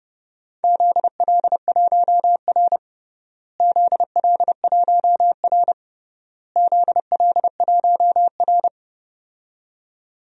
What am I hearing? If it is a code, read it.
Morse code, heard as a keyed tone, about 31 wpm, 700 Hz. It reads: ZL1R ZL1R ZL1R
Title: DXCC-100 Call Signs - 4 Characters - Repeated 3x - 30wpm